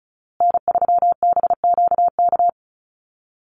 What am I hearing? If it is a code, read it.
Morse code, heard as a keyed tone, about 34 wpm, 700 Hz. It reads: N3BQK
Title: USA Call Signs Repeated 2x 35wpm